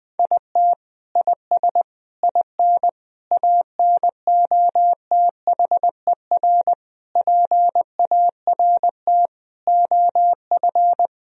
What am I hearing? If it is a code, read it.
Morse code, heard as a keyed tone, about 20 wpm, 700 Hz. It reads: IT IS IN ANOTHER PART OF
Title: Sentences from Top 1000 Words - Review Part 2 - 20wpm